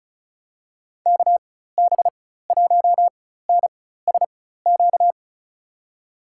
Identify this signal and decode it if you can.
Morse code, heard as a keyed tone, about 34 wpm, spaced as for 16 wpm, 700 Hz. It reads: KB1NSQ
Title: USA Call Signs - 4x Character Spacing - 35wpm